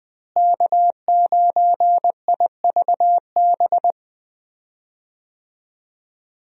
Morse code, heard as a keyed tone, about 20 wpm, 700 Hz. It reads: K9IVB
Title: USA Call Signs - 5 Characters - Repeated 3x - 20wpm